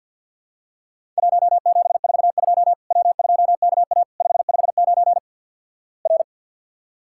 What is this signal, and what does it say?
Morse code, heard as a keyed tone, about 50 wpm, 700 Hz. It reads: 1742 W2KA 559 R
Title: Worked All Europe DX Contest - QTC Exchange - 50wpm